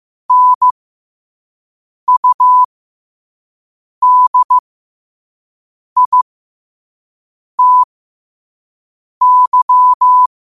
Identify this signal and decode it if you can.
Morse code, heard as a keyed tone, about 15 wpm, spaced as for 5 wpm, 1000 Hz. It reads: NUDITY
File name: qotdfiles / 5wpm